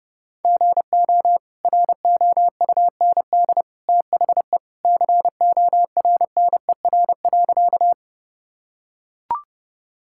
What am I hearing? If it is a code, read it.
Morse code, heard as a keyed tone, about 30 wpm, 700 Hz. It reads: GO ROUND THE CORNER.
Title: Sentences from Top 1600 Words 30wpm